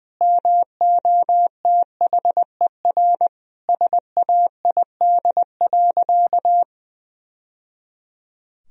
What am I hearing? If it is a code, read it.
Morse code, heard as a keyed tone, about 21 wpm, 700 Hz. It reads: MOTHER SAID.
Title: Sentences from Top 1000 Words - Review Part 3 - 20wpm